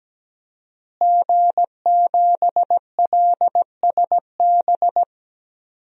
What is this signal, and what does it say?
Morse code, heard as a keyed tone, about 17 wpm, 700 Hz. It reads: G7LSB
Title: DXCC-100 Call Signs - 5 Characters - Repeated 2x - 17wpm